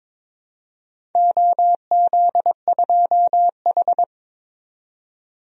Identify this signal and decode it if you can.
Morse code, heard as a keyed tone, about 21 wpm, 700 Hz. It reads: OZ2H
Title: DXCC-100 Call Signs - 4 Characters - 22wpm